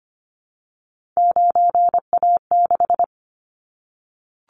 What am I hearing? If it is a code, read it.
Morse code, heard as a keyed tone, about 25 wpm, 700 Hz. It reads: 9A6
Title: Callsign Prefixes for Top 25 DXCC - Repeated 3x - 25wpm